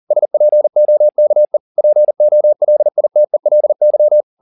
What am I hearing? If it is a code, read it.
Morse code, heard as a keyed tone, about 40 wpm, 600 Hz. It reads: SPOKE POLITELY